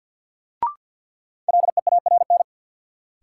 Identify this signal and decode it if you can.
Morse code, heard as a keyed tone, about 53 wpm, 700 Hz. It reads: LEARN
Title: Sentences from Top 300 Words - New Word Drill - 50wpm